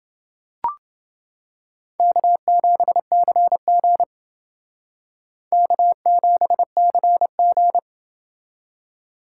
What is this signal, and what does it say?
Morse code, heard as a keyed tone, about 28 wpm, 700 Hz. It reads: K7CG K7CG